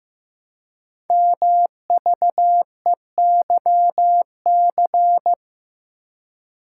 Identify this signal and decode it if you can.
Morse code, heard as a keyed tone, about 15 wpm, 700 Hz. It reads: MVEYC
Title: Code Groups - 5 Letters - 15wpm